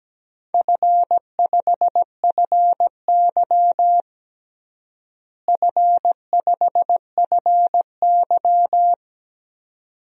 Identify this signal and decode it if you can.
Morse code, heard as a keyed tone, about 17 wpm, 700 Hz. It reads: F5FY F5FY
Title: DXCC-100 Call Signs - 4 Characters - Repeated 2x - 17wpm